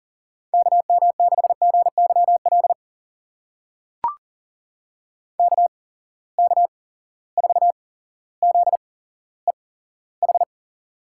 Morse code, heard as a keyed tone, about 38 wpm, spaced as for 17 wpm, 700 Hz. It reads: KM6GYL KK4ZEH
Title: USA Call Signs - 6 Characters - 8x Character Spacing - 40wpm